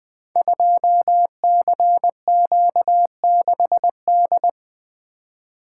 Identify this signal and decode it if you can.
Morse code, heard as a keyed tone, about 20 wpm, 700 Hz. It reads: 2CQ6D
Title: Code Groups - 5 Letter:Numbers - 20wpm